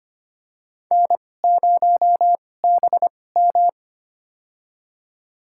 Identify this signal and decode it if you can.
Morse code, heard as a keyed tone, about 25 wpm, spaced as for 18 wpm, 700 Hz. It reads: N0BM
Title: USA Call Signs - 4 Characters - 2x Character Spacing - 25wpm